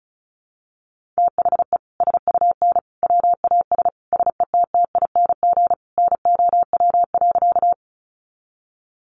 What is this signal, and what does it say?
Morse code, heard as a keyed tone, about 34 wpm, 700 Hz. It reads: THE SUN WAS SETTING NOW.
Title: Sentences from Top 1300 Words 35wpm